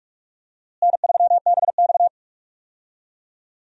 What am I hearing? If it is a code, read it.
Morse code, heard as a keyed tone, about 47 wpm, 700 Hz. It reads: N3BX
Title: USA Call Signs - 4 Characters - Repeated 2x - 45wpm